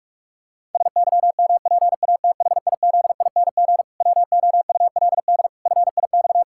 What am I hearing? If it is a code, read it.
Morse code, heard as a keyed tone, about 46 wpm, 700 Hz. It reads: SYMPATHIZING WOULD FIX
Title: Pangram Challenge v2 - 45wpm